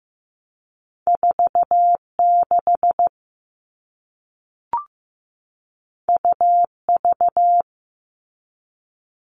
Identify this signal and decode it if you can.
Morse code, heard as a keyed tone, about 15 wpm, 700 Hz. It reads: 46 UV
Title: All Cut Numbers - 2-Digit 15wpm